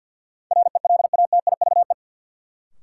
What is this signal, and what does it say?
Morse code, heard as a keyed tone, about 53 wpm, 700 Hz. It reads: RELATIVE
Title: Sentences from Top 1400 Words - New Word Drill Alt - 50wpm